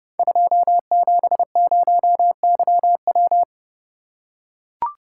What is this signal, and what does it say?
Morse code, heard as a keyed tone, about 30 wpm, 700 Hz. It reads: 270YW
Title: Code Groups - 5 Letter:Numbers - 30wpm